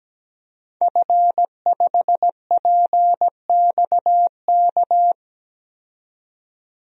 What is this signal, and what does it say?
Morse code, heard as a keyed tone, about 17 wpm, 700 Hz. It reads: F5PXK
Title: DXCC-100 Call Signs - 5 Characters - 17wpm